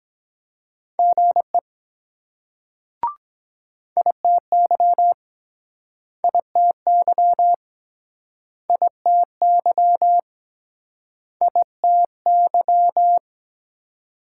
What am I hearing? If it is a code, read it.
Morse code, heard as a keyed tone, about 22 wpm, 700 Hz. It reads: GE ITY ITY ITY ITY